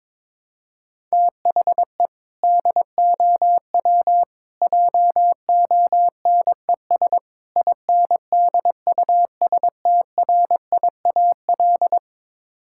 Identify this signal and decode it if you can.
Morse code, heard as a keyed tone, about 22 wpm, 700 Hz. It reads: THE DOW JONES INDUSTRIAL
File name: practice / 22wpm / Build A Phrase - 5-Word Phrases - 22wpm